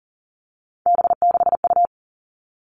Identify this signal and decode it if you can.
Morse code, heard as a keyed tone, about 40 wpm, 700 Hz. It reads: B6U
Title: Three Challenging Letter-Number Combinations 40wpm